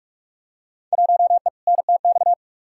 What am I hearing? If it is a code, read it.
Morse code, heard as a keyed tone, about 45 wpm, 700 Hz. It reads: 1E NTX
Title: Field Day Exchanges - 45wpm